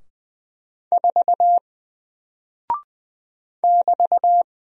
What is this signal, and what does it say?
Morse code, heard as a keyed tone, about 20 wpm, 700 Hz. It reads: 4 <BT>
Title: Single Character (Letter:Number:Punctuation:Prosign) 20wpm